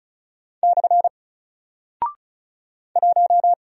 Morse code, heard as a keyed tone, about 36 wpm, 700 Hz. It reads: / 1
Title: Single Character (Letter:Number:Punctuation:Prosign) Alt 35wpm